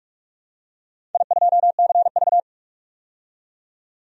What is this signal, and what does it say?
Morse code, heard as a keyed tone, about 44 wpm, 700 Hz. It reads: I2CV
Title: DXCC-100 Call Signs - 4 Characters - 45wpm